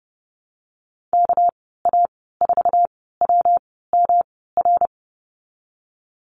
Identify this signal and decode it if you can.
Morse code, heard as a keyed tone, about 31 wpm, spaced as for 17 wpm, 700 Hz. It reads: KA4WMR
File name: Lesson 37g - Letter x - Call Signs 30wpm